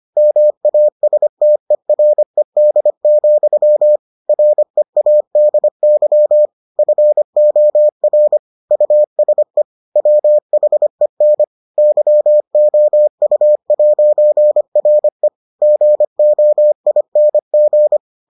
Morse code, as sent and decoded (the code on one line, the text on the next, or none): -- .- ... - . .-. -.. --..-- / .-. .- -.. -.-- / ..-. --- .-. / ..- ... / .-- .... -. / -.-- --- ..- .----. .-. / --. --- .. -. --.
MASTERED, READY FOR USE WHEN YOU'RE GOING